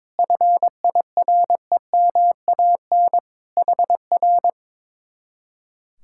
Morse code, heard as a keyed tone, about 22 wpm, 700 Hz. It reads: FIREMAN HR
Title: Speed-Racing Common QSO Phrases - 22wpm